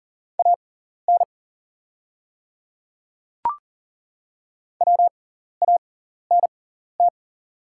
Morse code, heard as a keyed tone, about 40 wpm, spaced as for 14 wpm, 690 Hz. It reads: AN WANT
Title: 6x Character Spacing - Top 200 Words - 40wpm